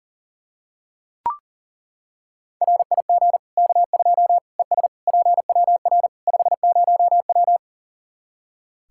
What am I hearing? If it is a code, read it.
Morse code, heard as a keyed tone, about 40 wpm, 700 Hz. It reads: RIG K2 ES PWR 50W
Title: Practice QSO - Exchange 2 - Info - 40wpm